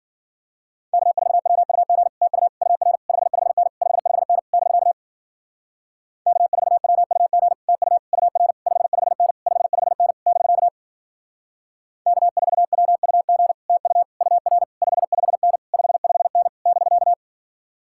Morse code, heard as a keyed tone, about 49 wpm, 700 Hz. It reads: K4WUG TU UR 55N 55N <BK> K4WUG TU UR 55N 55N <BK> K4WUG TU UR 55N 55N <BK>
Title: Speed-Racing Common QSO Phrases - 40wpm